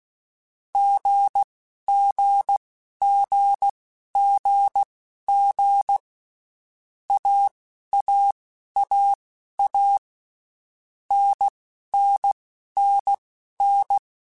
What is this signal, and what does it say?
Morse code, heard as a keyed tone, about 16 wpm, spaced as for 11 wpm, 800 Hz. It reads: GGGGG AAAA NNNN
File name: Basic Letter Groups By7Groups / AWUNDG / AWUNDG@10